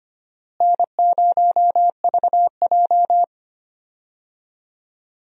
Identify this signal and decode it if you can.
Morse code, heard as a keyed tone, about 25 wpm, 700 Hz. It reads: N0VJ
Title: USA Call Signs - 4 Characters - Repeated 2x - 25wpm